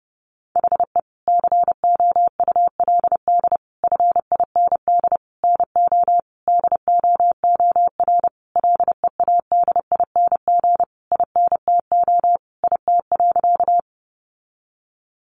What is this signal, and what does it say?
Morse code, heard as a keyed tone, about 30 wpm, 700 Hz. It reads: HE COULD FIND NO DOOR LEADING INTO IT.
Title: Sentences from Top 1100 Words 30wpm